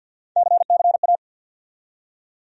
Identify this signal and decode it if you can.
Morse code, heard as a keyed tone, about 47 wpm, 700 Hz. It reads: CCA
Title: USA Call Sign Postfixes - 3 Letters 50wpm